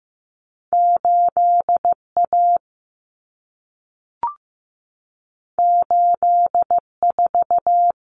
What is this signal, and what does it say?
Morse code, heard as a keyed tone, about 15 wpm, 700 Hz. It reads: 8 A 8 4
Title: Common Cut Numbers - 2-Digit 15wpm